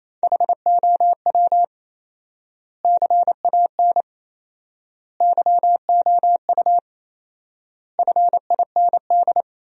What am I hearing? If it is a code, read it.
Morse code, heard as a keyed tone, about 28 wpm, 700 Hz. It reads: HOW CAN YOU FIND
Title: 4x Word-Spacing - Sentences from Top 200 Words - 28wpm